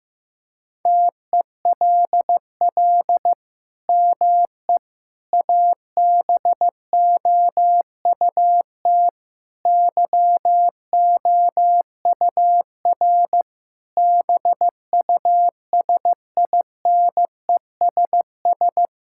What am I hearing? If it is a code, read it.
Morse code, heard as a keyed tone, about 15 wpm, 700 Hz. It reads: TELL ME ABOUT YOUR BUSINESS
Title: Sentences from Top 1000 Words - Review Part 5 - 15wpm